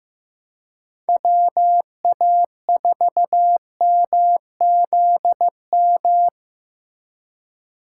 Morse code, heard as a keyed tone, about 15 wpm, 700 Hz. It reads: WA4MZM
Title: USA Call Signs - 6 Characters - 4x Character Spacing - 15wpm